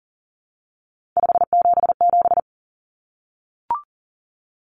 Morse code, heard as a keyed tone, about 40 wpm, 700 Hz. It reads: 577
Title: Numbers - 3-Digit 40wpm